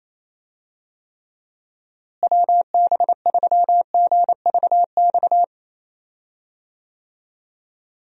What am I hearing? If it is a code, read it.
Morse code, heard as a keyed tone, about 28 wpm, 700 Hz. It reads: WB3GVX
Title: USA Call Signs - 6 Characters - Repeated 2x - 28wpm